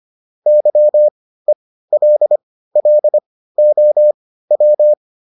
Morse code, heard as a keyed tone, about 25 wpm, spaced as for 15 wpm, 600 Hz. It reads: YELLOW